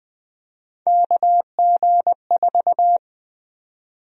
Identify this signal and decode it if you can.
Morse code, heard as a keyed tone, about 20 wpm, 700 Hz. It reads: KG4
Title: USA Call Sign Prefixes - 20wpm